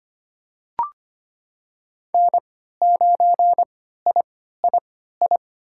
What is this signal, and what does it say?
Morse code, heard as a keyed tone, about 25 wpm, spaced as for 14 wpm, 700 Hz. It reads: N9III
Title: USA Call Signs - 5 Characters - 3x Character Spacing - 25wpm